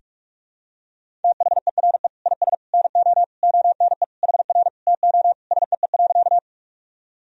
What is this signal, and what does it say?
Morse code, heard as a keyed tone, about 44 wpm, 700 Hz. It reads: THERE IS NO ONE HR TO SEE.
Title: Sentences from Top 200 Words and Abbreviations 45wpm